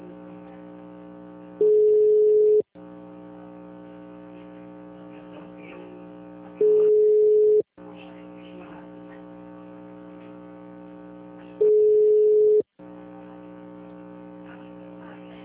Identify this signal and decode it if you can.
Ringback tone (Europe, 425 Hz), with speech.